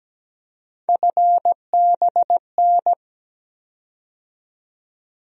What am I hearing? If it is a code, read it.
Morse code, heard as a keyed tone, about 17 wpm, 700 Hz. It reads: FBN
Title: USA Call Sign Postfixes - 3 Letters 17wpm